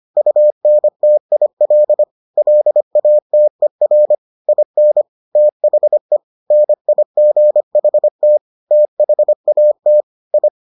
Morse code, heard as a keyed tone, about 25 wpm, 600 Hz. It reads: UNTIL LATER IN THE NIGHT THAT I